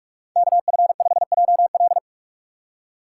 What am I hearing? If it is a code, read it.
Morse code, heard as a keyed tone, about 44 wpm, 700 Hz. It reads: KF5JL
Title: USA Call Signs - 5 Characters - 8x Character Spacing - 45wpm